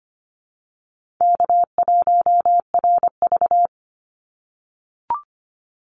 Morse code, heard as a keyed tone, about 25 wpm, 700 Hz. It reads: K1RV
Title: Easy Call Signs - 25wpm